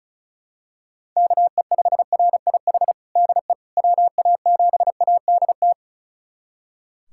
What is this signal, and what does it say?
Morse code, heard as a keyed tone, about 36 wpm, 700 Hz. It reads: KE5RIH DE WA7ADT
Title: USA Callsign DE Callsign - 6 Characters - 35wpm